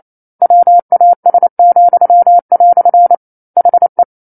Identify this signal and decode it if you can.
Morse code, heard as a keyed tone, about 28 wpm, 700 Hz. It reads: WAS," HE
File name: 09-Chapter-09 28 WPM